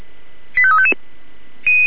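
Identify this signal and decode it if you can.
Five-tone selective call (ZVEI), heard as quick runs of five tones, tones of about 70 ms. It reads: 85355
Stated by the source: Received on an SDR, ZVEI five-tone call 85369.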